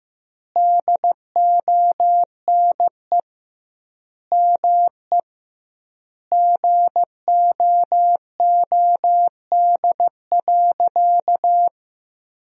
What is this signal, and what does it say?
Morse code, heard as a keyed tone, about 16 wpm, 700 Hz. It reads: DONE ME GOOD.
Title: 2x Word-Spacing - Sentences from Top 500 Words - 15wpm